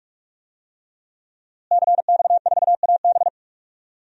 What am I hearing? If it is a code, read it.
Morse code, heard as a keyed tone, about 41 wpm, 700 Hz. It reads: CX4AB
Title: DXCC-100 Call Signs - 5 Characters - Repeated 3x - 45wpm